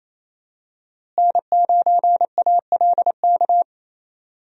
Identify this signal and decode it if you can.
Morse code, heard as a keyed tone, about 28 wpm, 700 Hz. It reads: N9ALK
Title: USA Call Signs - 5 Characters - Repeated 3x - 28wpm